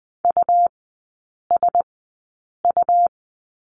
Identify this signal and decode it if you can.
Morse code, heard as a keyed tone, about 20 wpm, spaced as for 8 wpm, 700 Hz. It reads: USU
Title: 100 Lesson 17e - Letter u - Three Characters 20wpm